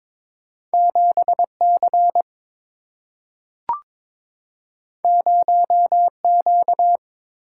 Morse code, heard as a keyed tone, about 22 wpm, 700 Hz. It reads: 7 C 0 Q
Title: Single-Letter Single-Number 22wpm